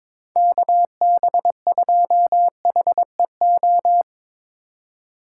Morse code, heard as a keyed tone, about 23 wpm, 700 Hz. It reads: KB2HEO
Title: Speed Racing - USA 2x3 Call Signs 22wpm